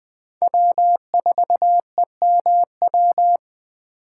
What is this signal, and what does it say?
Morse code, heard as a keyed tone, about 20 wpm, 700 Hz. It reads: W4EMW